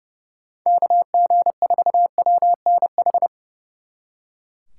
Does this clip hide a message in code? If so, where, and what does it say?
Yes: Morse code from 0.7 to 3.3 s, heard as a keyed tone, about 40 wpm, 700 Hz. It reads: KG4WNH